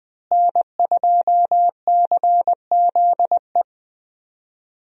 Morse code, heard as a keyed tone, about 20 wpm, 700 Hz. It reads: N2CZE
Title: USA Call Signs - 5 Characters - Repeated 2x - 20wpm